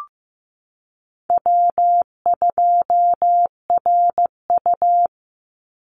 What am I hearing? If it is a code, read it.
Morse code, heard as a keyed tone, about 15 wpm, 700 Hz. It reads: W2RU
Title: Call Signs Repeated 3x - 15wpm